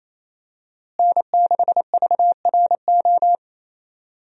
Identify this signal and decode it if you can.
Morse code, heard as a keyed tone, about 28 wpm, 700 Hz. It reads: N6VRO
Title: USA Call Signs - 5 Characters - Repeated 3x - 28wpm